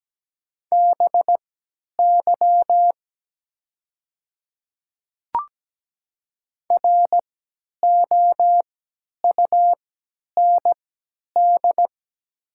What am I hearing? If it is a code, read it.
Morse code, heard as a keyed tone, about 17 wpm, spaced as for 10 wpm, 700 Hz. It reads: BY ROUND
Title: 3x Character Spacing - Top 500 Words - 17wpm